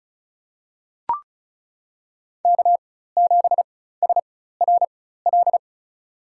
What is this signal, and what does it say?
Morse code, heard as a keyed tone, about 34 wpm, spaced as for 16 wpm, 700 Hz. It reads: K7SRL
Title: USA Call Signs - 5 Characters - 4x Character Spacing - 35wpm